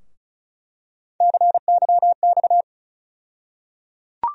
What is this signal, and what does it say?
Morse code, heard as a keyed tone, about 36 wpm, 700 Hz. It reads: CYX
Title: USA Call Sign Postfixes - 3 Letters 35wpm